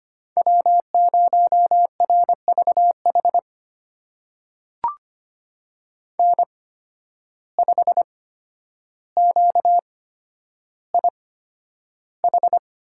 Morse code, heard as a keyed tone, about 25 wpm, spaced as for 11 wpm, 700 Hz. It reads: W0RVH N5QIH
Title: USA Call Signs - 8x Character Spacing - 25wpm